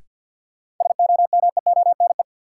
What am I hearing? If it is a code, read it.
Morse code, heard as a keyed tone, about 52 wpm, 700 Hz. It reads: SOMEONE
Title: Sentences from Top 600 Words - New Word Drill - 50wpm